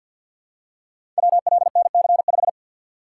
Morse code, heard as a keyed tone, about 51 wpm, 700 Hz. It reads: WLNC5
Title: Code Groups - 5 Letter:Numbers - 50wpm